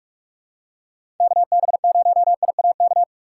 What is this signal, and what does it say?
Morse code, heard as a keyed tone, about 43 wpm, 700 Hz. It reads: KB0IAK
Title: USA Call Signs - 6 Characters - 4x Character Spacing - 45wpm